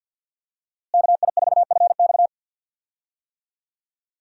Morse code, heard as a keyed tone, about 51 wpm, 700 Hz. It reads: KI4FX
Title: USA Call Signs - 5 Characters - Repeated 3x - 50wpm